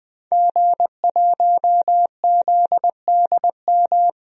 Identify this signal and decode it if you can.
Morse code, heard as a keyed tone, about 20 wpm, 700 Hz. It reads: G1ZDM
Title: DXCC-100 Call Signs - 5 Characters - 20wpm